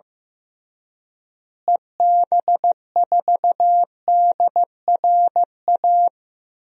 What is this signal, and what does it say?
Morse code, heard as a keyed tone, about 15 wpm, 700 Hz. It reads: EB4DRA
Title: DXCC-100 Call Signs - 6 Characters - Repeated 3x - 15wpm